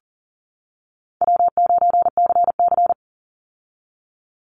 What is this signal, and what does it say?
Morse code, heard as a keyed tone, about 40 wpm, 700 Hz. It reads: W9CC
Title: Easy Call Signs - 40wpm